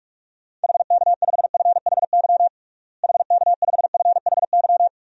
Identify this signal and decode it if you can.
Morse code, heard as a keyed tone, about 41 wpm, 700 Hz. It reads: HK5FHY HK5FHY
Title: DXCC-100 Call Signs - 6 Characters - Repeated 2x - 45wpm